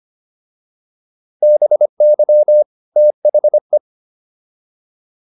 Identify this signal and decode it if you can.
Morse code, heard as a keyed tone, about 25 wpm, 600 Hz. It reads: BY THE